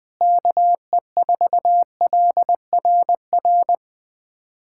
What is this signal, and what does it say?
Morse code, heard as a keyed tone, about 20 wpm, 700 Hz. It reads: KE4LRR
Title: USA Callsign DE Callsign 20wpm